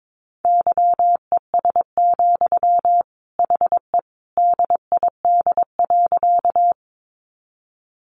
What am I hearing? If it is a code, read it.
Morse code, heard as a keyed tone, about 22 wpm, 700 Hz. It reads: YES, HE DID.
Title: Sentences from Top 500 Words 22wpm